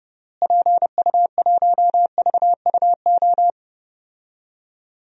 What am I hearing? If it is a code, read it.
Morse code, heard as a keyed tone, about 32 wpm, 700 Hz. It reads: PU1VUO
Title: DXCC-100 Call Signs - 6 Characters - Repeated 3x - 30wpm